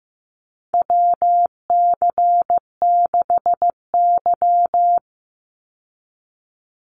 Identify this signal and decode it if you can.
Morse code, heard as a keyed tone, about 15 wpm, 700 Hz. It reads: WC6Y
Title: Call Signs Repeated 2x - 15wpm